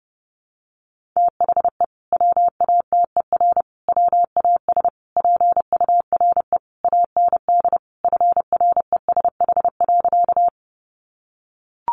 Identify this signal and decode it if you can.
Morse code, heard as a keyed tone, about 30 wpm, 700 Hz. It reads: THE WATER WAS PURE AND FRESH.
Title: Sentences from Top 1500 Words 30wpm